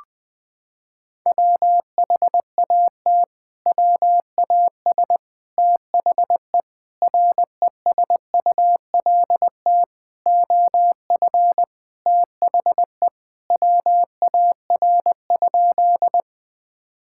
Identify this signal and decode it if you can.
Morse code, heard as a keyed tone, about 20 wpm, 700 Hz. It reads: WHAT WAS THE RESULT OF THE WAR?
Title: Sentences from Top 1000 Words - Review Part 5 - 20wpm